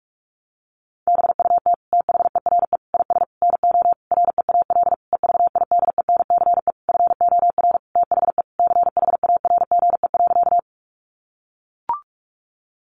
Morse code, heard as a keyed tone, about 44 wpm, 700 Hz. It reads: BUT THERE IS NO REAL EVIDENCE FOR THE CHARGE.
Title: Sentences from Top 1300 Words 45wpm